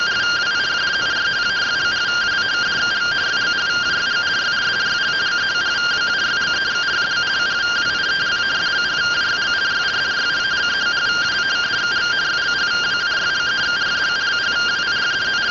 Two-tone teleprinter signal, about 170 Hz shift.